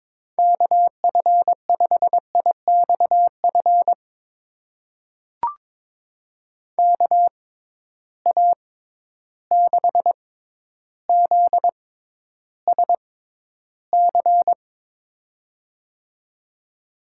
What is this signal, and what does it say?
Morse code, heard as a keyed tone, about 22 wpm, spaced as for 11 wpm, 700 Hz. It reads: KF5IXF KA6ZSC
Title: USA Call Signs - 6 Characters - 6x Character Spacing - 22wpm